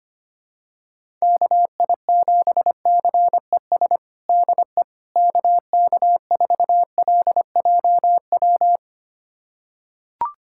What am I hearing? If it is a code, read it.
Morse code, heard as a keyed tone, about 25 wpm, 700 Hz. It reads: KI7CES DE KK4LJW
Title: USA Callsign DE Callsign 25wpm